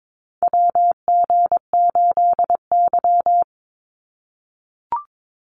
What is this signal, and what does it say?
Morse code, heard as a keyed tone, about 22 wpm, 700 Hz. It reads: WG8Y
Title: NA SOTA Chaser Callsigns 22wpm